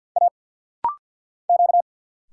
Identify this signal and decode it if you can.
Morse code, heard as a keyed tone, about 50 wpm, 690 Hz. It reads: A Q
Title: Single Letter-Number - Rapid-Fire 50wpm